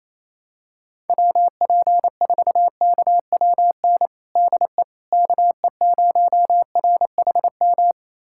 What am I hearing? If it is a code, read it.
Morse code, heard as a keyed tone, about 28 wpm, 700 Hz. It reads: WP4KWN DE KE0RHM
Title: USA Callsign DE Callsign - 6 Characters - 28wpm